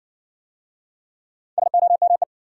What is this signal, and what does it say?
Morse code, heard as a keyed tone, about 55 wpm, 700 Hz. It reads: SOME